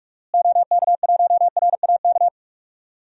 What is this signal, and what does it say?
Morse code, heard as a keyed tone, about 42 wpm, 700 Hz. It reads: OK1RAK
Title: DXCC-100 Call Signs - 6 Characters - Repeated 3x - 45wpm